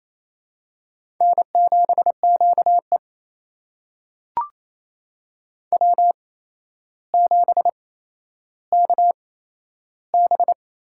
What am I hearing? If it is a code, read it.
Morse code, heard as a keyed tone, about 28 wpm, spaced as for 12 wpm, 700 Hz. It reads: N7QE W7KB
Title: USA Call Signs - 4 Characters - 8x Character Spacing - 28wpm